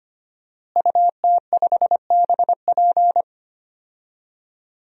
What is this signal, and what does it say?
Morse code, heard as a keyed tone, about 25 wpm, 700 Hz. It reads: UT5BP
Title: DXCC-100 Call Signs - 5 Characters - Repeated 3x - 25wpm